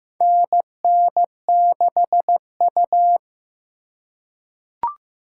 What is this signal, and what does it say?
Morse code, heard as a keyed tone, about 15 wpm, 700 Hz. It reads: NN6U
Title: USA Call Signs - 2x Character Spacing - 15wpm